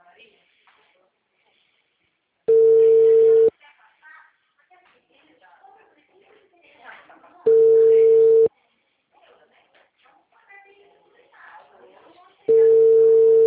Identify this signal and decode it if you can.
Ringback tone (China, 450 Hz), with speech.
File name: out- -2026